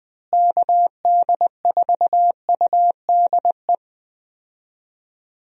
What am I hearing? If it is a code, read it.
Morse code, heard as a keyed tone, about 20 wpm, 700 Hz. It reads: KD4UDE
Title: USA Call Signs - 6 Characters - Repeated 3x - 20wpm